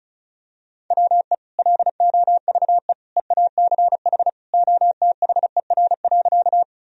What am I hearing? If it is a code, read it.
Morse code, heard as a keyed tone, about 34 wpm, 700 Hz. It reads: WE LOVE EACH OTHER.
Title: Sentences from Top 1000 Words - Review Part 1 - 35wpm